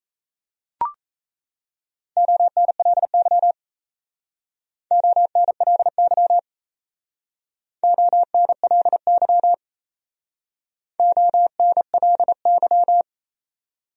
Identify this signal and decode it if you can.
Morse code, heard as a keyed tone, about 35 wpm, 700 Hz. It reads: ONLY ONLY ONLY ONLY
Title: Speed-Racing Top 200 Words 28wpm